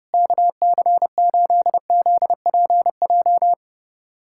Morse code, heard as a keyed tone, about 30 wpm, 700 Hz. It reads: KC8ZPJ
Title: USA Callsign DE Callsign - 6 Characters - 30wpm